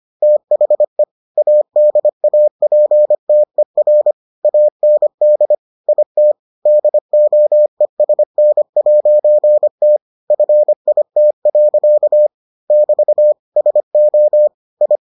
Morse code, heard as a keyed tone, about 25 wpm, 600 Hz. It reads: THE ADAPTER AND IT DOESN'T FIT. <BT> SO I